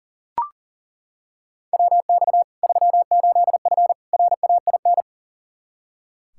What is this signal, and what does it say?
Morse code, heard as a keyed tone, about 40 wpm, 700 Hz. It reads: WX 38F RAIN
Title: Practice QSO - Exchange 2 - Info - 40wpm